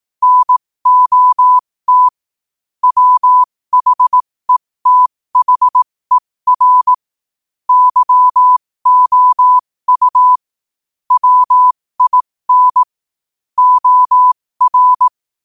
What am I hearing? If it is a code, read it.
Morse code, heard as a keyed tone, about 18 wpm, spaced as for 15 wpm, 1000 Hz. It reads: NOT WHETHER YOU WIN OR
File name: 2024-11-18-15wpm